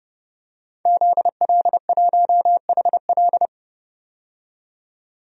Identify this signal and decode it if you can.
Morse code, heard as a keyed tone, about 30 wpm, 700 Hz. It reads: ZL1HL
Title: DXCC-100 Call Signs - 5 Characters - 30wpm